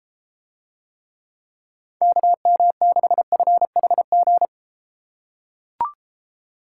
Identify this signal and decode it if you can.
Morse code, heard as a keyed tone, about 34 wpm, 700 Hz. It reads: KM6FHG